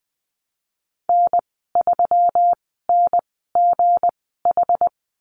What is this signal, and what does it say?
Morse code, heard as a keyed tone, about 20 wpm, spaced as for 15 wpm, 700 Hz. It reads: N3NGH
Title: 167 Lesson 27g - Number 7 - Call Signs 20wpm